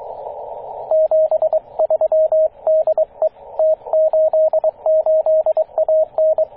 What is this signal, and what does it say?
Morse code, heard as a keyed tone, about 23 wpm, 630 Hz, with radio noise. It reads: 73 DE T88AN
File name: T88AN 14.024MHz CW